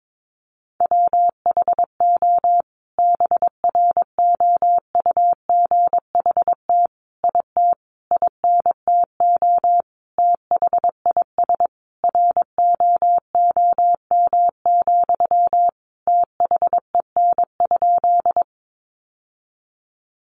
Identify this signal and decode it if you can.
Morse code, heard as a keyed tone, about 22 wpm, 700 Hz. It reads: WHO BROUGHT IT INTO THIS ROOM, THEN?
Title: Sentences from Top 600 Words 22wpm